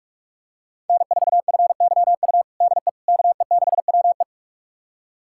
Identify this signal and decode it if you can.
Morse code, heard as a keyed tone, about 42 wpm, 700 Hz. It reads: N4FYU DE KE6WE
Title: USA Callsign DE Callsign - 5 Characters - 45wpm